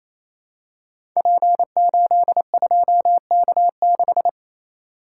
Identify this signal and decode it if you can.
Morse code, heard as a keyed tone, about 27 wpm, 700 Hz. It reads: P82K6
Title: Code Groups - 5 Letter:Numbers - 28wpm